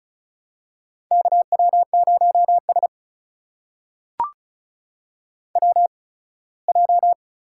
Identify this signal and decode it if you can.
Morse code, heard as a keyed tone, about 36 wpm, spaced as for 21 wpm, 700 Hz. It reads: KW0S WJ
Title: USA Call Signs - 8x Character Spacing - 35wpm